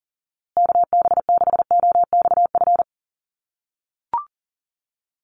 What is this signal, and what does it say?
Morse code, heard as a keyed tone, about 40 wpm, 700 Hz. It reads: KB6OXF
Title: USA Call Signs Repeated 2x 40wpm